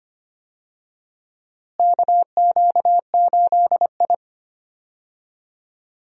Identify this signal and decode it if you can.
Morse code, heard as a keyed tone, about 25 wpm, 700 Hz. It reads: KQ8I
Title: USA Call Signs - 4 Characters - Repeated 3x - 25wpm